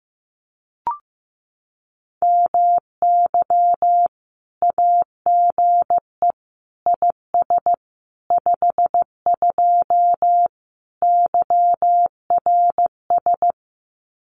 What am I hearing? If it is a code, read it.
Morse code, heard as a keyed tone, about 15 wpm, 700 Hz. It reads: MY AGE IS 52 YRS
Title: Common QSO Phrases Encore 15wpm